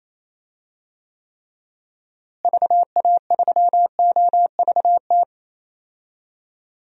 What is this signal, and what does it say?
Morse code, heard as a keyed tone, about 28 wpm, 700 Hz. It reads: VA3OVT
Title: DXCC-100 Call Signs - 6 Characters - Repeated 3x - 28wpm